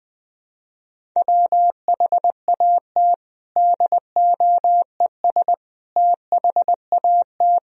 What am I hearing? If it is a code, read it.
Morse code, heard as a keyed tone, about 21 wpm, 700 Hz. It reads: WHAT DOES THAT
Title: Sentences from Top 1000 Words - Review Part 1 - 20wpm